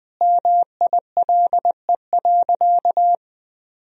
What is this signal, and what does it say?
Morse code, heard as a keyed tone, about 20 wpm, 700 Hz. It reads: MILE.
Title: Sentences from Top 1000 Words - Review Part 1 - 20wpm